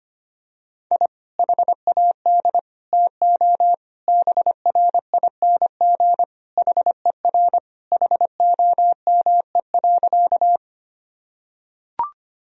Morse code, heard as a keyed tone, about 25 wpm, 700 Hz. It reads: I HAD TO BRING HER HOME.
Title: Sentences from Top 1000 Words - Review Part 3 - 25wpm